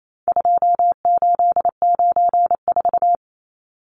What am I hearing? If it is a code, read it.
Morse code, heard as a keyed tone, about 28 wpm, 700 Hz. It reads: 2894
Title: Numbers - 4-Digit 28wpm